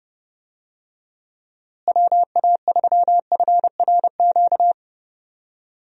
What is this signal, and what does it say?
Morse code, heard as a keyed tone, about 30 wpm, 700 Hz. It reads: WA3FRQ
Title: USA Call Signs - 6 Characters - Repeated 2x - 30wpm